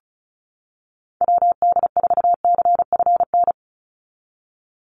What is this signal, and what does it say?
Morse code, heard as a keyed tone, about 34 wpm, 700 Hz. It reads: WD4CFN
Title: NA SOTA Chaser Callsigns 35wpm